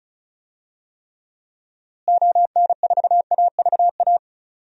Morse code, heard as a keyed tone, about 36 wpm, 700 Hz. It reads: ON4AVA
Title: DXCC-100 Call Signs - 6 Characters - Repeated 3x - 35wpm